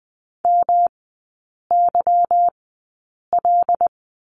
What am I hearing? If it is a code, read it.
Morse code, heard as a keyed tone, about 20 wpm, spaced as for 8 wpm, 700 Hz. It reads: MYL
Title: 172 Lesson 28e - Number 9 - Three Characters 20wpm